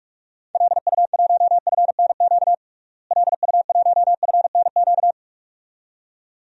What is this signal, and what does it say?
Morse code, heard as a keyed tone, about 47 wpm, 700 Hz. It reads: LU1FNQ LU1FNQ
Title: DXCC-100 Call Signs - 6 Characters - Repeated 2x - 45wpm